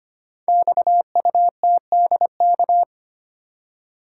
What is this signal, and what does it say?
Morse code, heard as a keyed tone, about 25 wpm, 700 Hz. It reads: XUTDK
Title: Code Groups - 5 Letters - 25wpm